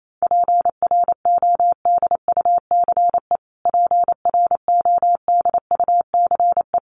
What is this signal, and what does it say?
Morse code, heard as a keyed tone, about 28 wpm, 700 Hz. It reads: PRODUCE PRODUCE
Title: Top 500 Words Repeated 28wpm